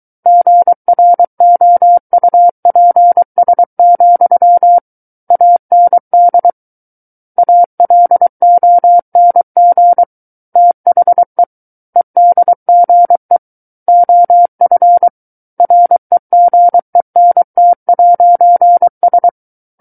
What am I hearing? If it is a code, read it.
Morse code, heard as a keyed tone, about 23 wpm, 700 Hz. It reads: GROUPS, AND ALONG THE EDGE OF REGENT'S
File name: Book One / 14-Chapter 14 - 23 WPM